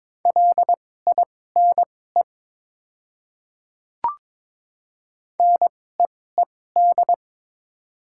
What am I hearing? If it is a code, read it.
Morse code, heard as a keyed tone, about 22 wpm, spaced as for 16 wpm, 690 Hz. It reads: LINE NEED
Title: 2x Character Spacing - Top 200 Words - 22wpm